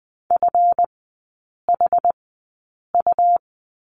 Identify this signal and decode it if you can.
Morse code, heard as a keyed tone, about 20 wpm, spaced as for 8 wpm, 700 Hz. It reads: FHU
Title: 138 Lesson 23e - Letter f - Three Characters 20wpm